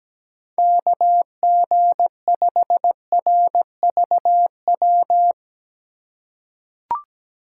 Morse code, heard as a keyed tone, about 17 wpm, 700 Hz. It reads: KG5RVW
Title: USA Call Signs - 2x Character Spacing - 17wpm